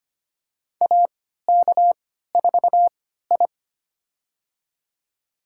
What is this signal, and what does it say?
Morse code, heard as a keyed tone, about 25 wpm, spaced as for 14 wpm, 700 Hz. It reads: AK4I
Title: USA Call Signs - 3x Character Spacing - 25wpm